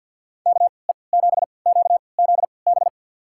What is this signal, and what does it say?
Morse code, heard as a keyed tone, about 47 wpm, spaced as for 28 wpm, 700 Hz. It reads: KE7QZB
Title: USA Call Signs - 6 Characters - 3x Character Spacing - 50wpm